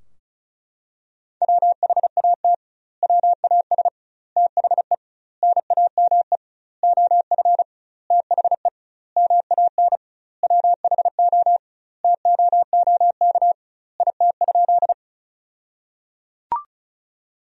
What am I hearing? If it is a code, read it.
Morse code, heard as a keyed tone, about 36 wpm, 700 Hz. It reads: WHAT WAS THE NAME OF THE MAN WHO TOOK IT?
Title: 2x Word-Spacing - Sentences from Top 300 Words - 35wpm